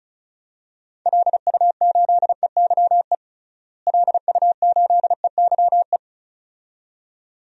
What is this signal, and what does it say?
Morse code, heard as a keyed tone, about 34 wpm, 700 Hz. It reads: LU8EYE LU8EYE
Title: DXCC-100 Call Signs - 6 Characters - Repeated 2x - 35wpm